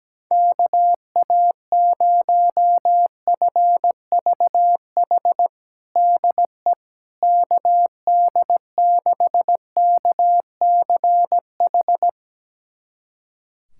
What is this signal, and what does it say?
Morse code, heard as a keyed tone, about 17 wpm, 700 Hz. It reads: KA0FVH DE KD6KCH
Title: USA Callsign DE Callsign - 6 Characters - 17wpm